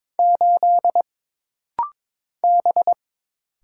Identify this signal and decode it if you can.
Morse code, heard as a keyed tone, about 22 wpm, 700 Hz. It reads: 8 B
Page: This rapid-fire timing is geared to push your ICR (Instant Character Recognition) skills.